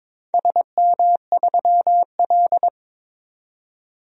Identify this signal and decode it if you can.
Morse code, heard as a keyed tone, about 22 wpm, 700 Hz. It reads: SM3L
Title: DXCC-100 Call Signs - 4 Characters - 22wpm